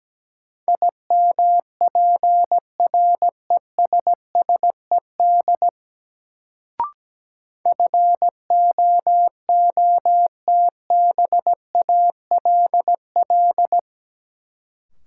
Morse code, heard as a keyed tone, about 17 wpm, 700 Hz. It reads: IMPRESSED FOOTBALL
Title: Sentences from Top 1600 Words - New Word Drill Alt - 17wpm